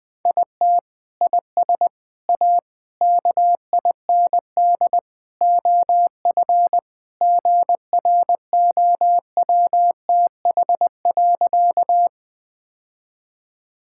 Morse code, heard as a keyed tone, about 21 wpm, 700 Hz. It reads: IT IS A KIND OF GROWTH.
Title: Sentences from Top 1000 Words - Review Part 5 - 20wpm